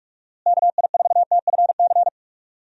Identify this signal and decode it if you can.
Morse code, heard as a keyed tone, about 41 wpm, 700 Hz. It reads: KI4TFC
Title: USA Call Signs - 6 Characters - Repeated 3x - 45wpm